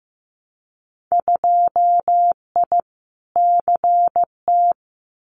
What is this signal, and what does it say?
Morse code, heard as a keyed tone, about 15 wpm, 700 Hz. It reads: 2I CT
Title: Winter Field Day Exchanges 15wpm